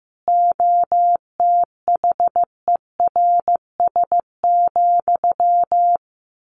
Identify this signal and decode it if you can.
Morse code, heard as a keyed tone, about 14 wpm, 700 Hz. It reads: OTHERS,
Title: A Sound Of Thunder 15wpm